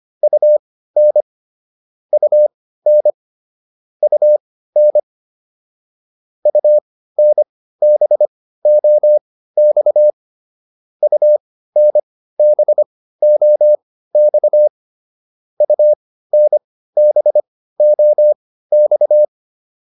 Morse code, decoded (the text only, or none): UN UN UN UNBOX UNBOX UNBOX